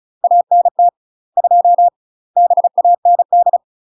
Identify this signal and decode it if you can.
Morse code, heard as a keyed tone, about 36 wpm, 700 Hz. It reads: ANT 2 BAND
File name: UK MASTER QSOs TRAINING / 35WPM / UK MASTER QSOs 0035